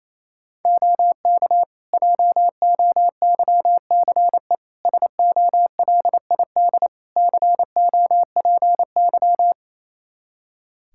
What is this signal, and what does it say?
Morse code, heard as a keyed tone, about 28 wpm, 700 Hz. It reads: OK JOYCE SOLID COPY
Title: Practice QSO - Exchange 3 - More Info - 28wpm